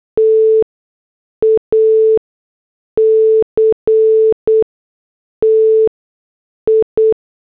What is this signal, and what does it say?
Morse code, heard as a keyed tone, about 8 wpm, 440 Hz. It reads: T A C T I
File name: morse / headlines